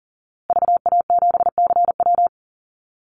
Morse code, heard as a keyed tone, about 40 wpm, 700 Hz. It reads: VA7CW
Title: Call Signs Repeated 3x - 40wpm